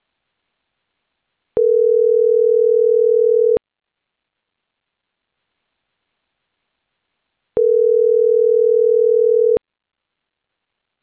Ringback tone (North America), 440 + 480 Hz.